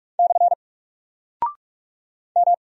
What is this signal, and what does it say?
Morse code, heard as a keyed tone, about 44 wpm, 700 Hz. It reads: / M
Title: Single Character (Letter:Number:Punctuation:Prosign) 45wpm